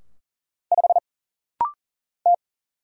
Morse code, heard as a keyed tone, about 42 wpm, 700 Hz. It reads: 5 T
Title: Single Letter-Number - Rapid-Fire 40wpm